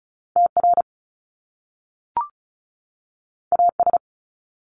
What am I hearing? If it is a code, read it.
Morse code, heard as a keyed tone, about 36 wpm, 700 Hz. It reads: TR AS
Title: Top Two Letter Combinations in Words 35wpm